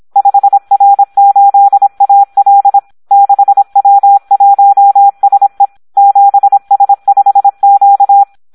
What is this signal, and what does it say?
Morse code, heard as a keyed tone, about 26 wpm, 800 Hz. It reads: 5R8AL 6W1SE 7S5Q
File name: Audio practica CW / SD 1 26